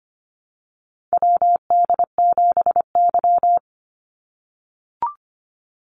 Morse code, heard as a keyed tone, about 25 wpm, 700 Hz. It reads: WD7Y
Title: NA SOTA Chaser Callsigns 25wpm